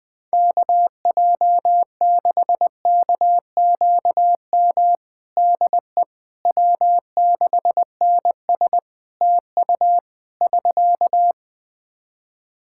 Morse code, heard as a keyed tone, about 20 wpm, 700 Hz. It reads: KJ6KQM DE W6NS TU <SK>
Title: Practice QSO - Exchange 4 - Ending - 20wpm